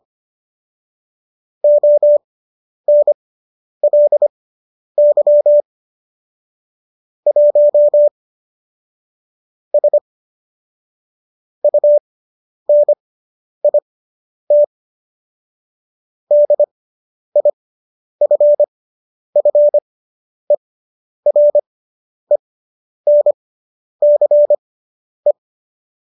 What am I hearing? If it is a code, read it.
Morse code, heard as a keyed tone, about 25 wpm, spaced as for 10 wpm, 590 Hz. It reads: ONLY 1 S UNIT DIFFERENCE